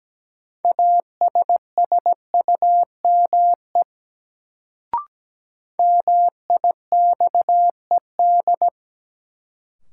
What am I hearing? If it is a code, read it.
Morse code, heard as a keyed tone, about 17 wpm, 700 Hz. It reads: ASSUME MIXED
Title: Sentences from Top 1500 Words - New Word Drill - 17wpm